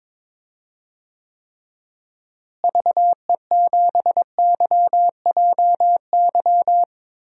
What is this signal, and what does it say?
Morse code, heard as a keyed tone, about 22 wpm, 700 Hz. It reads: VE7YJY
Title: DXCC-100 Call Signs - 6 Characters - Repeated 2x - 22wpm